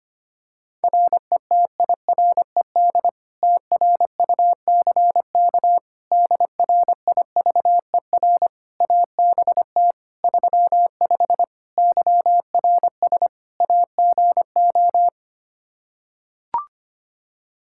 Morse code, heard as a keyed tone, about 25 wpm, 700 Hz. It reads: RETIRED TRUCK DRIVER ABT 35 YRS AGO
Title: Practice QSO - Exchange 3 - More Info - 25wpm